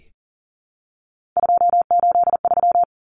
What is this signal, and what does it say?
Morse code, heard as a keyed tone, about 40 wpm, 700 Hz. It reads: 283
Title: Common Cut Numbers - 3-Digit 40wpm